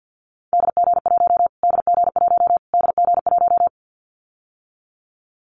Morse code, heard as a keyed tone, about 50 wpm, 700 Hz. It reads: DG1 DG1 DG1
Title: Callsign Prefixes for Top 25 DXCC - Repeated 3x - 50wpm